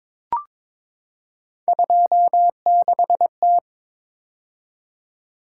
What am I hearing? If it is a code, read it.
Morse code, heard as a keyed tone, about 23 wpm, 700 Hz. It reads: 26T
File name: practice / 22wpm / Three Letter-Number Combinations - 22wpm